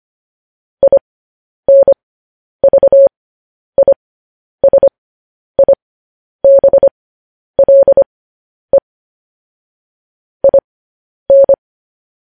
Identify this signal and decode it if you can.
Morse code, heard as a keyed tone, about 25 wpm, spaced as for 10 wpm, 550 Hz. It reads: INVISIBLE IN
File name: podcast / News headlines-1x-25x10